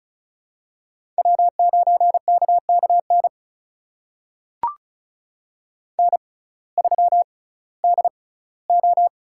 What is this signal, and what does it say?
Morse code, heard as a keyed tone, about 36 wpm, spaced as for 19 wpm, 700 Hz. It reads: W9KKN N3DO
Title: USA Call Signs - 5 Characters - 6x Character Spacing - 35wpm